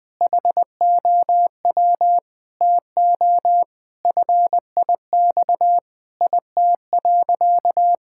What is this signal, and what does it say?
Morse code, heard as a keyed tone, about 20 wpm, 700 Hz. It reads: HOW TO FIX IT.
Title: Sentences with 6-Word Phrases - 20wpm